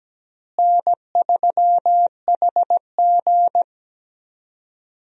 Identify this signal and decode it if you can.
Morse code, heard as a keyed tone, about 17 wpm, 700 Hz. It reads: N3HG
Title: USA Call Signs - 4 Characters - Repeated 3x - 17wpm